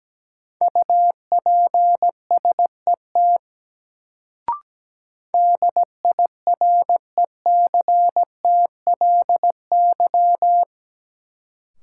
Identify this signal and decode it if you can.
Morse code, heard as a keyed tone, about 17 wpm, 700 Hz. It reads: UPSET DIRECTLY
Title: Sentences from Top 800 Words - New Word Drill - 17wpm